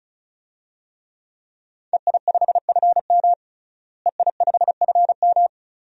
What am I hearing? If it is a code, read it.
Morse code, heard as a keyed tone, about 36 wpm, 700 Hz. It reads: EI5FM EI5FM
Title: DXCC-100 Call Signs - 5 Characters - Repeated 2x - 35wpm